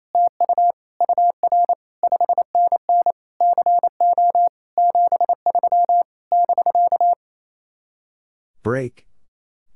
Morse code, heard as a keyed tone, about 28 wpm, 700 Hz. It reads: TU UR 5NN CO 73 <BK>
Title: Speed-Racing Common QSO Phrases - 28wpm